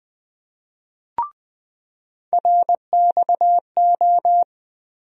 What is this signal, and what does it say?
Morse code, heard as a keyed tone, about 20 wpm, 700 Hz. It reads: RXO
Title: USA Call Sign Postfixes - 3 Letters 20wpm